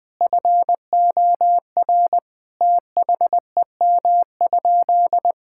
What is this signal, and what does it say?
Morse code, heard as a keyed tone, about 20 wpm, 700 Hz. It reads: FOR THEM?
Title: Sentences from Top 1000 Words - Review Part 5 - 20wpm